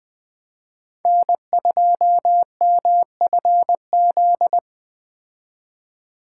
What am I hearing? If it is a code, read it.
Morse code, heard as a keyed tone, about 20 wpm, 700 Hz. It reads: N2MFZ
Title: Speed Racing - USA 1x3 Call Signs 17wpm